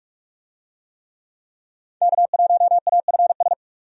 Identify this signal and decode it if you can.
Morse code, heard as a keyed tone, about 46 wpm, 700 Hz. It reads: K1AFS
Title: USA Call Signs - 5 Characters - 45wpm